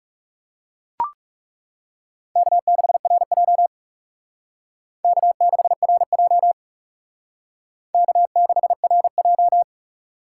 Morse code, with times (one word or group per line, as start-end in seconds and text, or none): 2.4-3.7 K6RJ
5.0-6.5 K6RJ
7.9-9.6 K6RJ